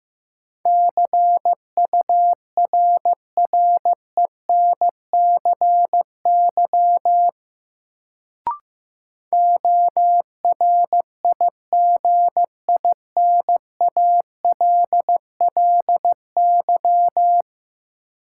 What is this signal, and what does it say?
Morse code, heard as a keyed tone, about 15 wpm, 700 Hz. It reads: CURRENCY ORIGINALLY
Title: Sentences from Top 1500 Words - New Word Drill - 15wpm